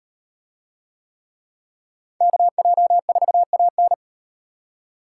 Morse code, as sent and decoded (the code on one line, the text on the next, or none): -.- .--- ....- .- -.
KJ4AN